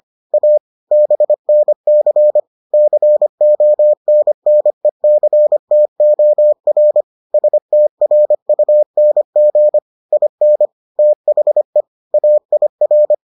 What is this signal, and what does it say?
Morse code, heard as a keyed tone, about 25 wpm, 600 Hz. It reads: A BNC CONNECTOR STRUNG IN THE AIR